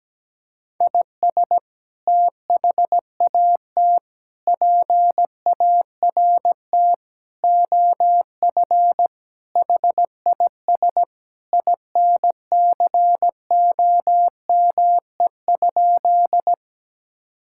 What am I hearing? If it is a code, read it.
Morse code, heard as a keyed tone, about 17 wpm, 700 Hz. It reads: IS THAT PART OF HIS INCOME?
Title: Sentences from Top 1000 Words - Review Part 5 - 17wpm